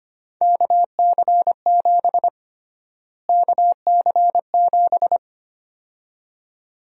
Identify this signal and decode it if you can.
Morse code, heard as a keyed tone, about 25 wpm, 700 Hz. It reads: KC7 KC7
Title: USA Call Sign Prefixes - Repeated 2x - 25wpm